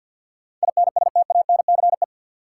Morse code, heard as a keyed tone, about 52 wpm, 700 Hz. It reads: INSTANCE
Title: Sentences from Top 800 Words - New Word Drill - 50wpm